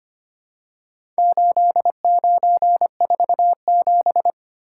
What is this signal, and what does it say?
Morse code, heard as a keyed tone, about 25 wpm, 690 Hz. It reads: 8947
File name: practice / 25wpm / Four Letter-Number Combinations - 25wpm